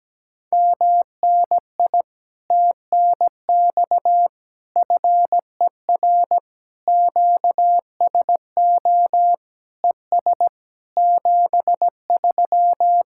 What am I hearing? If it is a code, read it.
Morse code, heard as a keyed tone, about 17 wpm, 700 Hz. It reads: MNI TNX FER QSO ES 73
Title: Speed-Racing Common QSO Phrases Encore - 17wpm